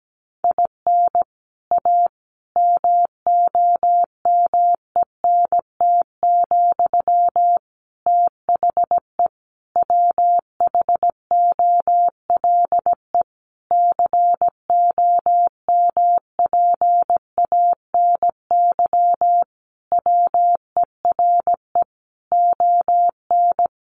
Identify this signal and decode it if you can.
Morse code, heard as a keyed tone, about 17 wpm, 700 Hz. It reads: IN A MOMENT, THE WHOLE COMPANY WERE ON
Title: Sentences from Top 1000 Words 17wpm